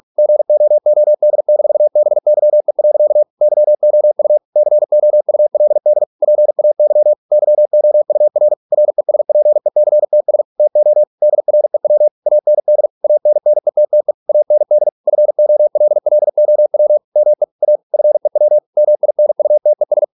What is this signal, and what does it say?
Morse code, heard as a keyed tone, about 46 wpm, 600 Hz. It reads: GOOD-BYE. YOU COULD PAY YOUR RESPECTS TO DREW AND ANNETTE AND FOLLOW ME A FEW MINUTES